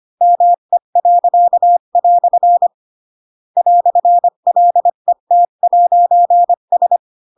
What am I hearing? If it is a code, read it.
Morse code, heard as a keyed tone, about 25 wpm, 700 Hz. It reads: ME." "LET'S